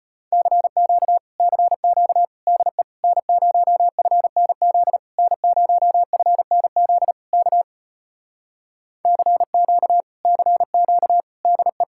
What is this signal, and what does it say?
Morse code, heard as a keyed tone, about 33 wpm, 700 Hz. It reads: CQ CQ DE N0FNZ N0FNZ K CQ CQ DE